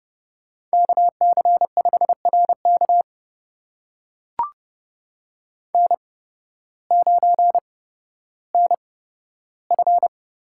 Morse code, heard as a keyed tone, about 30 wpm, spaced as for 14 wpm, 700 Hz. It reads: KC5RK N9NF
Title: USA Call Signs - 5 Characters - 8x Character Spacing - 30wpm